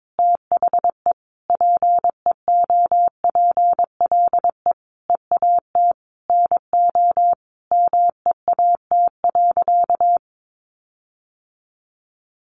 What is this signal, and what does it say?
Morse code, heard as a keyed tone, about 22 wpm, 700 Hz. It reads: THE PEOPLE EAT NO MEAT.
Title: Sentences from Top 1300 Words 22wpm